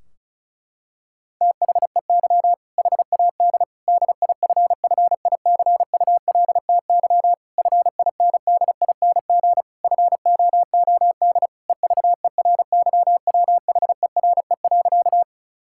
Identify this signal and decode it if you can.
Morse code, heard as a keyed tone, about 36 wpm, 700 Hz. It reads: THEY HAD DIFFICULTY FINDING FOOD EVERYWHERE.
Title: Sentences from Top 1600 Words 35wpm